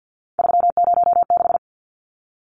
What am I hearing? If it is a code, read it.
Morse code, heard as a keyed tone, about 50 wpm, 700 Hz. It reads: 306
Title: Numbers - 3-Digit 50wpm